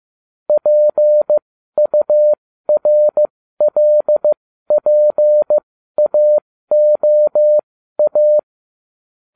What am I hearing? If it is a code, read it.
Morse code, heard as a keyed tone, about 15 wpm, spaced as for 12 wpm, 600 Hz. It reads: PURLPAOA